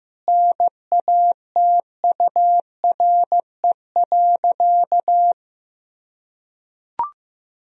Morse code, heard as a keyed tone, about 15 wpm, 700 Hz. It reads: NATURE.
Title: Sentences from Top 1000 Words - Review Part 3 - 15wpm